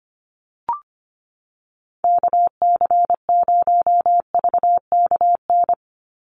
Morse code, heard as a keyed tone, about 25 wpm, 700 Hz. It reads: KC0VKN
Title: Call Signs Repeated 2x - 25wpm